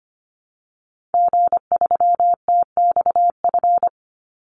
Morse code, heard as a keyed tone, about 25 wpm, 700 Hz. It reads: G3TXF
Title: Call Signs Repeated 3x - 25wpm